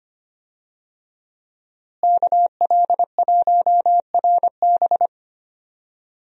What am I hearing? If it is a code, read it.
Morse code, heard as a keyed tone, about 25 wpm, 700 Hz. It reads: KL1RB
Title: USA Call Signs - 5 Characters - Repeated 3x - 25wpm